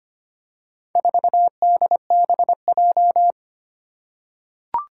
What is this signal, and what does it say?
Morse code, heard as a keyed tone, about 25 wpm, 700 Hz. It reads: 4DBJ
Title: Four Challenging Letter-Number Combinations 25wpm